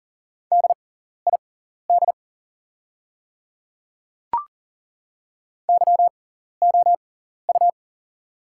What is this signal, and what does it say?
Morse code, heard as a keyed tone, about 40 wpm, spaced as for 14 wpm, 700 Hz. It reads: DID YOU
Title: 6x Character Spacing - Top 100 Words - 40wpm